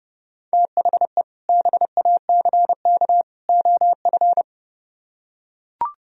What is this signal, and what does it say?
Morse code, heard as a keyed tone, about 30 wpm, 700 Hz. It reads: THE BACK OF